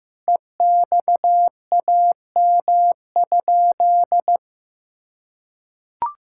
Morse code, heard as a keyed tone, about 15 wpm, 700 Hz. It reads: EXAM?
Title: Sentences from Top 1000 Words - Review Part 6 - 15wpm